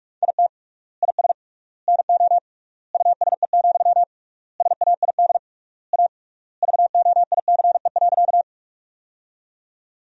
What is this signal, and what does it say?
Morse code, heard as a keyed tone, about 41 wpm, 700 Hz. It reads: IT IS NO USE, SAID A VOICE.
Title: 3x Word-Spacing - Sentences from Top 500 Words - 45wpm